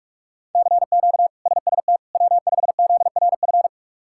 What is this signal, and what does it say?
Morse code, heard as a keyed tone, about 41 wpm, 700 Hz. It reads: CQ SST W5ZRF
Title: K1USN SST QSO Practice - 45wpm